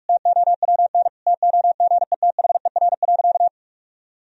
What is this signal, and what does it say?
Morse code, heard as a keyed tone, about 46 wpm, 700 Hz. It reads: TOWN TOGETHER.
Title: Sentences from Top 1000 Words - Review Part 2 - 45wpm